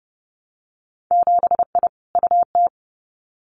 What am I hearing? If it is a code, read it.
Morse code, heard as a keyed tone, about 30 wpm, 700 Hz. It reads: 7I UT
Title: Winter Field Day Exchanges Encore 30wpm